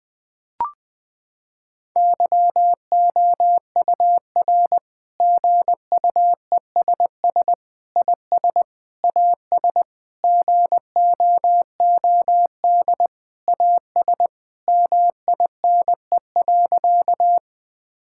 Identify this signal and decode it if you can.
Morse code, heard as a keyed tone, about 21 wpm, 700 Hz. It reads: YOUR GUESS IS AS GOOD AS MINE.
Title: Sentences with 7-Word Phrases - 20wpm